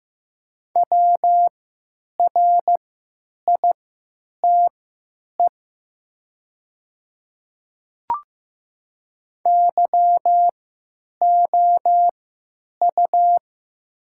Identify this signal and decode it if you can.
Morse code, heard as a keyed tone, about 15 wpm, spaced as for 9 wpm, 700 Hz. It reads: WRITE YOU
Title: 3x Character Spacing - Top 300 Words - 15wpm